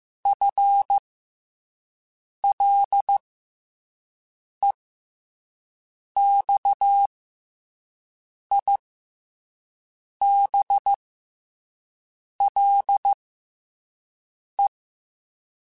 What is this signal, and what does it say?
Morse code, heard as a keyed tone, about 15 wpm, spaced as for 5 wpm, 800 Hz. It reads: FLEXIBLE
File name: FL Runs / FL Runs mp3s / FL Runs @5